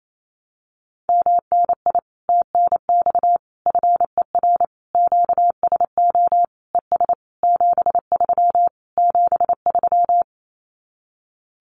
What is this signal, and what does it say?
Morse code, heard as a keyed tone, about 28 wpm, 700 Hz. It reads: MNI TNX FER QSO ES 73 73
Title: Common QSO Phrases Encore 28wpm